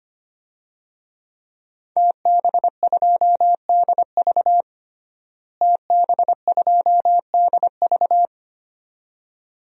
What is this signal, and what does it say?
Morse code, heard as a keyed tone, about 25 wpm, 700 Hz. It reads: TB2DV TB2DV
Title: DXCC-100 Call Signs - 5 Characters - Repeated 2x - 25wpm